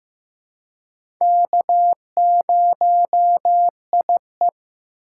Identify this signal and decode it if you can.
Morse code, heard as a keyed tone, about 15 wpm, 700 Hz. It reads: K0IE
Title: USA Call Signs - 2x Character Spacing - 15wpm